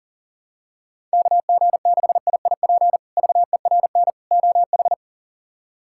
Morse code, heard as a keyed tone, about 40 wpm, 700 Hz. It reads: KG6IIP VERN OH
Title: K1USN SST QSO Practice - 40wpm